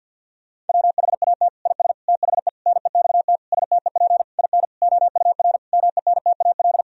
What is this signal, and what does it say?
Morse code, heard as a keyed tone, about 50 wpm, 700 Hz. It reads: WHAT IS THE NEXT STEP IN OUR MENTAL
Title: Sentences from Top 1000 Words - Review Part 4 - 50wpm